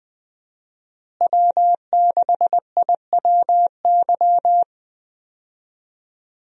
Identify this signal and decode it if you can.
Morse code, heard as a keyed tone, about 20 wpm, 700 Hz. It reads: W6IWY
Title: USA Call Signs - 5 Characters - Repeated 2x - 20wpm